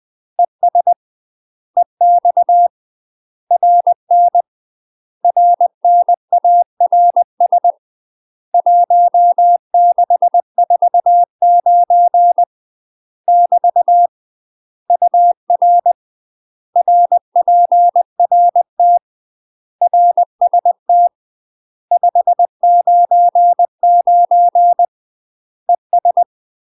Morse code, heard as a keyed tone, about 20 wpm, 700 Hz. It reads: ES EX RN RNARS 1649 <BT> UR RPRT RST 599 ES
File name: UK MASTER QSOs TRAINING/20WPM